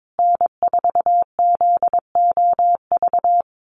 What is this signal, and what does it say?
Morse code, heard as a keyed tone, about 22 wpm, 700 Hz. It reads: N4ZOV
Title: USA Call Signs Repeated 3x 22wpm